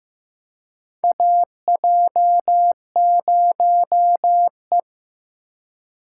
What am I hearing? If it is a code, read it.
Morse code, heard as a keyed tone, about 15 wpm, 700 Hz. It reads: AJ0E
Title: USA Call Signs - 4 Characters - 2x Character Spacing - 15wpm